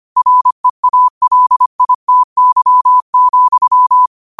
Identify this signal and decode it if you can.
Morse code, heard as a keyed tone, about 25 wpm, 1000 Hz. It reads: REALITY,
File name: qotdfiles / 25wpm